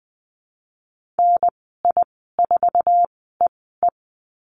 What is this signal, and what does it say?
Morse code, heard as a keyed tone, about 20 wpm, spaced as for 15 wpm, 700 Hz. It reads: NI4EE
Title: 048 Lesson 9g - Number 4 - Call Signs 20wpm